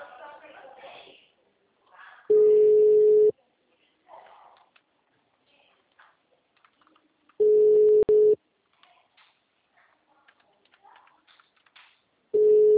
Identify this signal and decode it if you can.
Ringback tone (Europe, 425 Hz), with speech.
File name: out- -2013- - - .46627